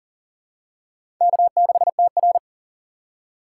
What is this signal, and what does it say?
Morse code, heard as a keyed tone, about 43 wpm, 700 Hz. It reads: K6TR
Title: USA Call Signs - 4 Characters - 4x Character Spacing - 40wpm